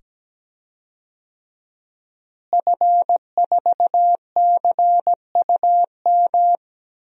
Morse code, heard as a keyed tone, about 17 wpm, 700 Hz. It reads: F4CUM
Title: DXCC-100 Call Signs - 5 Characters - Repeated 2x - 17wpm